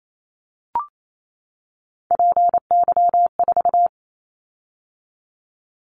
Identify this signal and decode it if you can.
Morse code, heard as a keyed tone, about 28 wpm, 700 Hz. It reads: PY4
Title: Callsign Prefixes for Top 100 DXCC - 28wpm